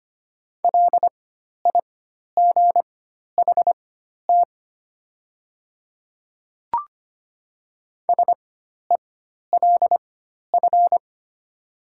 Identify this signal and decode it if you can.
Morse code, heard as a keyed tone, about 25 wpm, spaced as for 12 wpm, 700 Hz. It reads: LIGHT SELF
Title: 4x Character Spacing - Top 200 Words - 25wpm